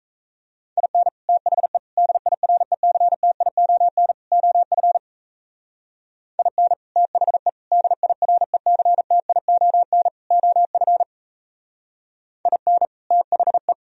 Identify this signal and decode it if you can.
Morse code, heard as a keyed tone, about 35 wpm, 700 Hz. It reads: IN THE DIRECTION OF IN THE DIRECTION OF IN THE
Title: Speed-Racing - Sets of 4 Words 28wpm